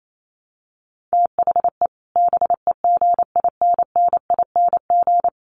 Morse code, heard as a keyed tone, about 28 wpm, 700 Hz. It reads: THE BEGINNING
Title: Sets of 4 Words 28wpm